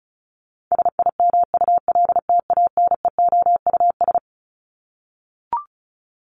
Morse code, heard as a keyed tone, about 34 wpm, 700 Hz. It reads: SIMULTANEOUS
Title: LY Suffix - Top 100 Words with Base Words 35wpm